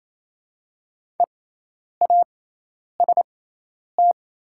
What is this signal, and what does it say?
Morse code, heard as a keyed tone, about 28 wpm, spaced as for 10 wpm, 700 Hz. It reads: EAST
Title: 6x Character Spacing - Top 500 Words - 28wpm